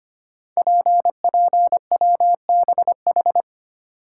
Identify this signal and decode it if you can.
Morse code, heard as a keyed tone, about 25 wpm, 700 Hz. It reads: PPWBH
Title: Code Groups - 5 Letters - 25wpm